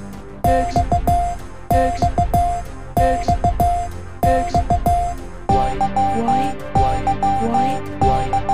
Morse code, heard as a keyed tone, about 15 wpm, spaced as for 12 wpm, 700 Hz, with noise. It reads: XXXX E E